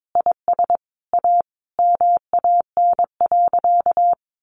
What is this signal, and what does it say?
Morse code, heard as a keyed tone, about 22 wpm, 700 Hz. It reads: IS A MAN.
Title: Sentences from Top 500 Words 22wpm